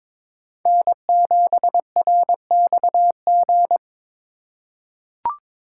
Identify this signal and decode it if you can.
Morse code, heard as a keyed tone, about 22 wpm, 700 Hz. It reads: N7RXG
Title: USA Call Signs - 5 Characters - 2x Character Spacing - 22wpm